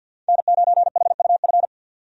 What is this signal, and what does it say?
Morse code, heard as a keyed tone, about 49 wpm, 700 Hz. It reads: N9HUF
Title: USA Call Signs - 5 Characters - 50wpm